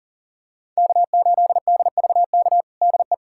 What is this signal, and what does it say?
Morse code, heard as a keyed tone, about 40 wpm, 700 Hz. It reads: K8DVK DE
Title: USA Callsign DE Callsign 40wpm